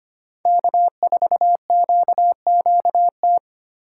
Morse code, heard as a keyed tone, about 25 wpm, 700 Hz. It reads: K4QQT
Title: USA Call Signs - 5 Characters - 3x Character Spacing - 25wpm